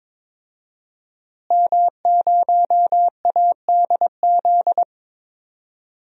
Morse code, heard as a keyed tone, about 22 wpm, 700 Hz. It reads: M0ADZ
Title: DXCC-100 Call Signs - 5 Characters - Repeated 3x - 22wpm